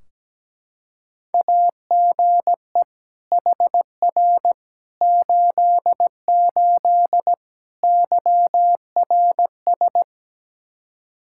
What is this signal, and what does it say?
Morse code, heard as a keyed tone, about 17 wpm, 700 Hz. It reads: AGE HR 88 YRS
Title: Practice QSO - Exchange 3 - More Info - 17wpm